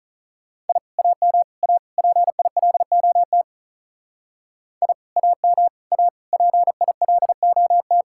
Morse code, heard as a keyed tone, about 42 wpm, 700 Hz. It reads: I AM A PILOT I AM A PILOT
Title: Speed-Racing Common QSO Phrases Encore - 35wpm